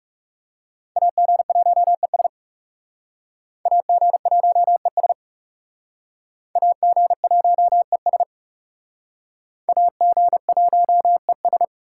Morse code, heard as a keyed tone, about 36 wpm, 700 Hz. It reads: AG1ES AG1ES AG1ES AG1ES
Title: Speed Racing - USA 2x2 Call Signs 30wpm